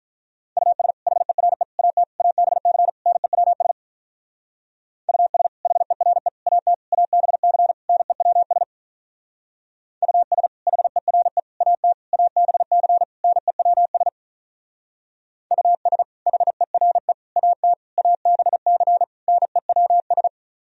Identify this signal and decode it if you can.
Morse code, heard as a keyed tone, about 44 wpm, 700 Hz. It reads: US HERE AT ABC NEWS US HERE AT ABC NEWS US HERE AT ABC NEWS US HERE AT ABC NEWS